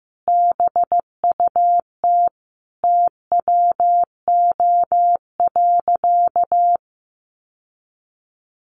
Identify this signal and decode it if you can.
Morse code, heard as a keyed tone, about 15 wpm, 700 Hz. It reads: BUT TWO.
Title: Sentences from Top 1400 Words 15wpm